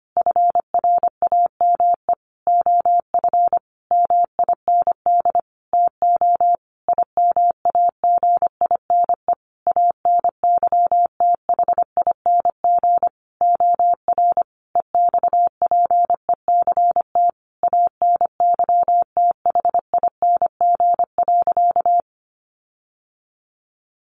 Morse code, heard as a keyed tone, about 25 wpm, 700 Hz. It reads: FRAME OF MIND TO IMAGINE ANYTHING OR EXPECT ANYTHING.
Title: Sentences from Top 1400 Words 25wpm